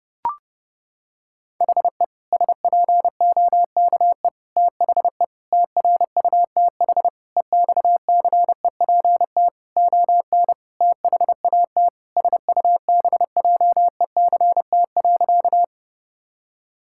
Morse code, heard as a keyed tone, about 30 wpm, 700 Hz. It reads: HE SPOKE THE TRUTH EXCEPT ON THAT SUBJECT.
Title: Sentences from Top 1600 Words 30wpm